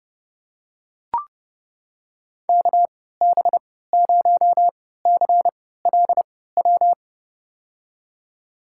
Morse code, heard as a keyed tone, about 32 wpm, spaced as for 17 wpm, 700 Hz. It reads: KB0CLW
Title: USA Call Signs - 3x Character Spacing - 30wpm